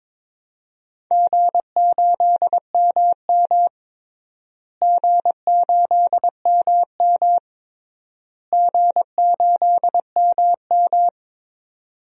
Morse code, heard as a keyed tone, about 22 wpm, 700 Hz. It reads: G8MM G8MM G8MM
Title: DXCC-100 Call Signs - 4 Characters - Repeated 3x - 22wpm